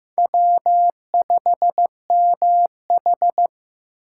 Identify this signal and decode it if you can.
Morse code, heard as a keyed tone, about 15 wpm, 700 Hz. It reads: W5MH
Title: USA Call Signs - 4 Characters - 6x Character Spacing - 15wpm